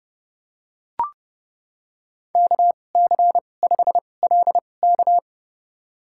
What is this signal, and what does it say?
Morse code, heard as a keyed tone, about 30 wpm, spaced as for 22 wpm, 700 Hz. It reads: KC5LK
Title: USA Call Signs - 2x Character Spacing - 30wpm